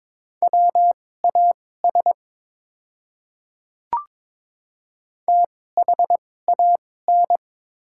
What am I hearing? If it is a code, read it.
Morse code, heard as a keyed tone, about 23 wpm, spaced as for 16 wpm, 700 Hz. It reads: WAS THAN
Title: 2x Character Spacing - Top 200 Words - 22wpm